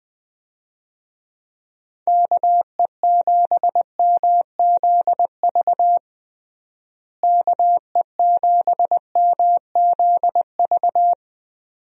Morse code, heard as a keyed tone, about 20 wpm, 700 Hz. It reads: KE7MZV KE7MZV
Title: USA Call Signs - 6 Characters - Repeated 2x - 20wpm